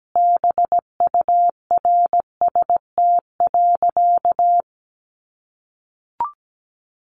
Morse code, heard as a keyed tone, about 17 wpm, 700 Hz. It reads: BURST.
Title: A Sound Of Thunder 17wpm